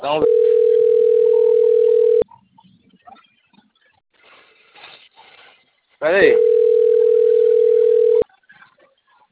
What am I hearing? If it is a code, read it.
Ringback tone (North America), 440 + 480 Hz, with speech.
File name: out- - - - - .97421